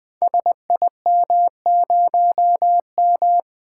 Morse code, heard as a keyed tone, about 20 wpm, 700 Hz. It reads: SIM0M
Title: Code Groups - 5 Letter:Numbers - 20wpm